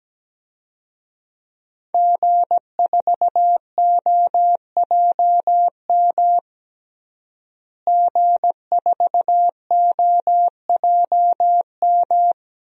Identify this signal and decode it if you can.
Morse code, heard as a keyed tone, about 17 wpm, 700 Hz. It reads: G4OJM G4OJM
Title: DXCC-100 Call Signs - 5 Characters - Repeated 3x - 17wpm